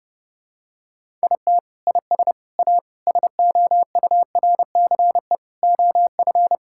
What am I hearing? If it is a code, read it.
Morse code, heard as a keyed tone, about 30 wpm, 700 Hz. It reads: IT IS A SOURCE OF
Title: Sentences from Top 1000 Words - Review Part 5 - 30wpm